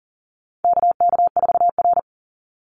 Morse code, heard as a keyed tone, about 40 wpm, 700 Hz. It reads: KK4R
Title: Call Signs - 40wpm